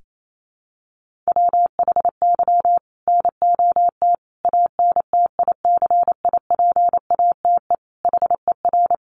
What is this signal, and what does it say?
Morse code, heard as a keyed tone, about 28 wpm, 700 Hz. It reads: WHY NOT ANTICIPATE HER
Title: Sentences from Top 1500 Words 28wpm